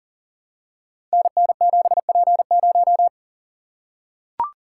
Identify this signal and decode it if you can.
Morse code, heard as a keyed tone, about 38 wpm, 700 Hz. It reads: NN7P0
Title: Code Groups - 5 Letter:Numbers - 40wpm